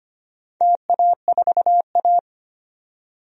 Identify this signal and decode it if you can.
Morse code, heard as a keyed tone, about 25 wpm, 700 Hz. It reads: TA4A
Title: DXCC-100 Call Signs - 4 Characters - 25wpm